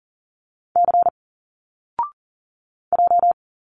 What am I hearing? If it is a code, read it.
Morse code, heard as a keyed tone, about 40 wpm, 700 Hz. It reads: C J
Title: Single Letters 40wpm